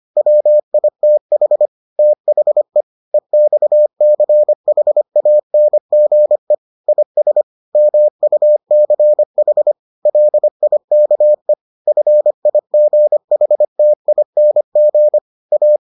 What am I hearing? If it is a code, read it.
Morse code, heard as a keyed tone, about 25 wpm, 600 Hz. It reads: WITH THE EXCHANGE IS MUCH LIKE FIGHTING A